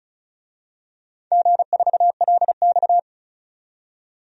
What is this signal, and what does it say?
Morse code, heard as a keyed tone, about 36 wpm, 690 Hz. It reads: G4LX
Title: DXCC-100 Call Signs - 4 Characters - 35wpm